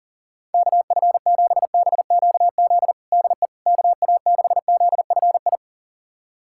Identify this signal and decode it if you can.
Morse code, heard as a keyed tone, about 40 wpm, 700 Hz. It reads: KF7BQZ DE KA6ZFI
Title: USA Callsign DE Callsign - 6 Characters - 40wpm